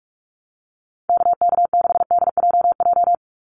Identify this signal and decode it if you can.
Morse code, heard as a keyed tone, about 44 wpm, 700 Hz. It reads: KK6DJJ
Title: USA Call Signs 45wpm